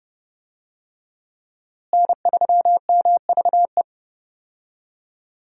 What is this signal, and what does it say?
Morse code, heard as a keyed tone, about 32 wpm, 700 Hz. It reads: N3MVE
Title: USA Call Signs - 5 Characters - Repeated 3x - 30wpm